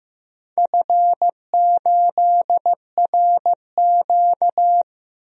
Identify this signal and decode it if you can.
Morse code, heard as a keyed tone, about 15 wpm, 700 Hz. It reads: F8RQ
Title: DXCC-100 Call Signs - 4 Characters - Repeated 3x - 15wpm